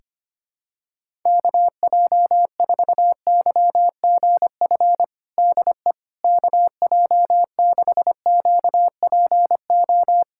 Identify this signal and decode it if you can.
Morse code, heard as a keyed tone, about 25 wpm, 700 Hz. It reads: KJ4YGF DE KJ6QPO
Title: USA Callsign DE Callsign 25wpm